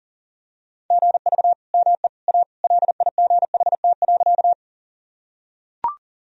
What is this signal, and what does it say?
Morse code, heard as a keyed tone, about 40 wpm, 700 Hz. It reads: GV ME A LIGHT.
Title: Sentences from Top 200 Words and Abbreviations 40wpm